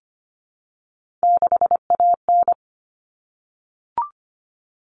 Morse code, heard as a keyed tone, about 25 wpm, 700 Hz. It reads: 6AN
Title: Common Cut Numbers - 3-Digit 25wpm